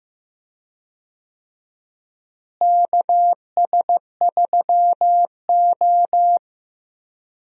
Morse code, heard as a keyed tone, about 15 wpm, 700 Hz. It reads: KS3O
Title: USA Call Signs - 4 Characters - Repeated 3x - 15wpm